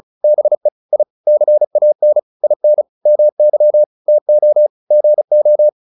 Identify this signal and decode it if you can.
Morse code, heard as a keyed tone, about 34 wpm, 600 Hz. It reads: BE I CAN IN MY TO GO